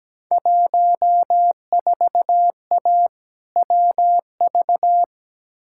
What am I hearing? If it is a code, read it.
Morse code, heard as a keyed tone, about 17 wpm, 700 Hz. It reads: 14A WV
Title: Field Day Exchanges Encore - 17wpm